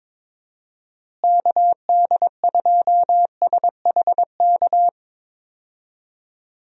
Morse code, heard as a keyed tone, about 22 wpm, 700 Hz. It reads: KD2SHK
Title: USA Call Signs - 6 Characters - Repeated 3x - 22wpm